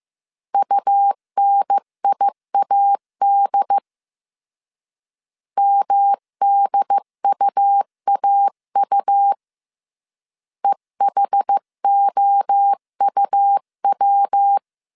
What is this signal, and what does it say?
Morse code, heard as a keyed tone, about 15 wpm, 800 Hz. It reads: UNIAD MDUAU EHOUW
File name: CW / DET / 1 3 1500C S C 14wpm 2458